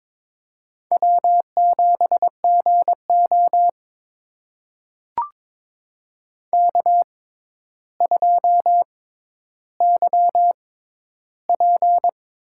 Morse code, heard as a keyed tone, about 22 wpm, spaced as for 11 wpm, 700 Hz. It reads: W7GO K2YP
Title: USA Call Signs - 4 Characters - 6x Character Spacing - 22wpm